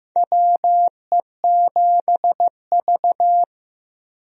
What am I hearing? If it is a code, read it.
Morse code, heard as a keyed tone, about 15 wpm, 700 Hz. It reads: WE7V
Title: USA Call Signs - 4 Characters - Repeated 2x - 15wpm